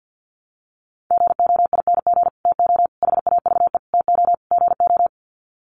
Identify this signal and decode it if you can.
Morse code, heard as a keyed tone, about 50 wpm, 700 Hz. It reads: GOING TO HAVE TO GO
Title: Sets of 5 Words Encore 50wpm